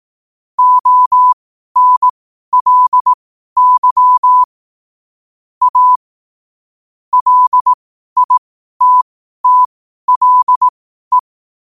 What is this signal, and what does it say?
Morse code, heard as a keyed tone, about 18 wpm, spaced as for 12 wpm, 1000 Hz. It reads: ONLY A LITTLE
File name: qotdfiles / 12wpm